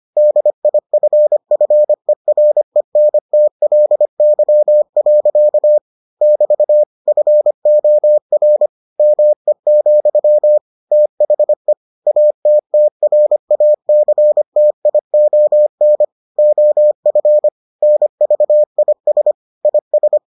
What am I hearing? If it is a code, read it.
Morse code, heard as a keyed tone, about 25 wpm, 600 Hz. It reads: DIFFERENTLY. <BT> FOR ME, THE ATTRACTION OF NVIS IS